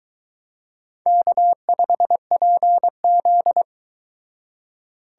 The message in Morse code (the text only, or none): K5PZ